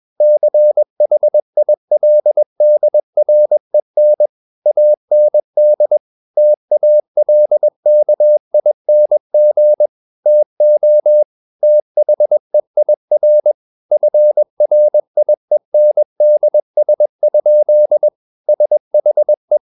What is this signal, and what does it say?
Morse code, heard as a keyed tone, about 21 wpm, 600 Hz. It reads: CHILDREN AND TALKING TO THEIR FRIENDS? SHE
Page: CW med Gnister